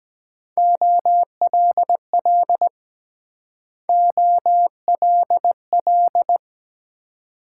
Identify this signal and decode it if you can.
Morse code, heard as a keyed tone, about 23 wpm, 700 Hz. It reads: OLL OLL